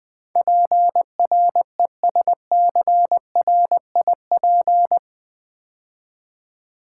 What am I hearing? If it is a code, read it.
Morse code, heard as a keyed tone, about 21 wpm, 700 Hz. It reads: PRESCRIP
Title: Build A Word - 4 Letters At A Time - 12-Letter Words - 20wpm